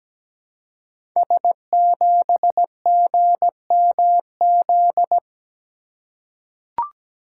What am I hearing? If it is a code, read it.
Morse code, heard as a keyed tone, about 17 wpm, 700 Hz. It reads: S7GMZ
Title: Code Groups - 5 Letter:Numbers - 17wpm